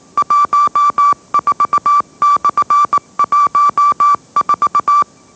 Morse code, heard as a keyed tone, about 18 wpm, 1160 Hz. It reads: 14/14
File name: mrx / mrx081 20wpm